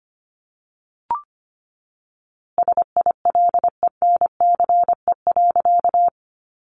Morse code, heard as a keyed tone, about 25 wpm, 700 Hz. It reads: SILENCE.
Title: A Sound Of Thunder 25wpm